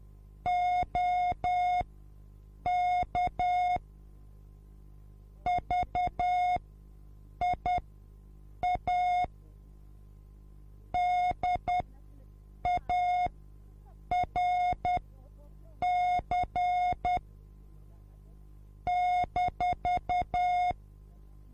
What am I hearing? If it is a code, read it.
Morse code, heard as a keyed tone, about 10 wpm, spaced as for 7 wpm, 720 Hz. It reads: OK VIA DARC -